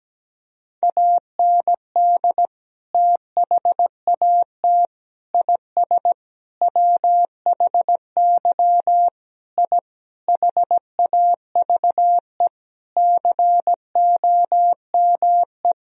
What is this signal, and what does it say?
Morse code, heard as a keyed tone, about 17 wpm, 700 Hz. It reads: AND THAT IS WHY I HAVE COME
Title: Sentences from Top 1000 Words - Review Part 3 - 17wpm